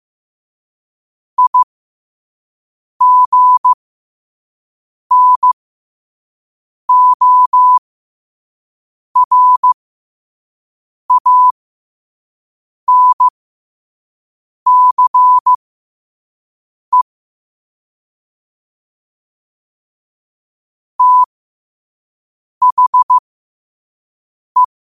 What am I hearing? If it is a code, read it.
Morse code, heard as a keyed tone, about 15 wpm, spaced as for 5 wpm, 1000 Hz. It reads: IGNORANCE THE
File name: qotdfiles / 5wpm